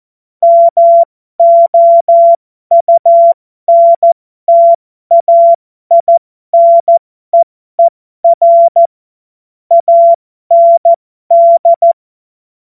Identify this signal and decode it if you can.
Morse code, heard as a keyed tone, about 14 wpm, spaced as for 12 wpm, 680 Hz. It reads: MOUNTAINEER AND
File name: Morsekurs / Witze